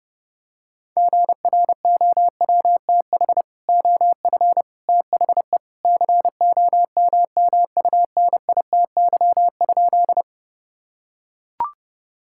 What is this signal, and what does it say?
Morse code, heard as a keyed tone, about 30 wpm, 700 Hz. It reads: GROWTH OF THE COMMUNITY?
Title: Sentences from Top 1000 Words - Review Part 5 - 30wpm